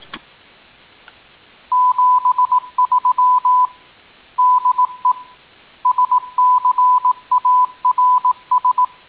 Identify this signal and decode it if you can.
Morse code, heard as a keyed tone, about 18 wpm, 1000 Hz. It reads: 73 DE SCARS